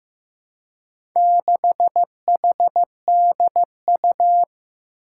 Morse code, heard as a keyed tone, about 15 wpm, 700 Hz. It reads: 6HDU
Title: Four Challenging Letter-Number Combinations 15wpm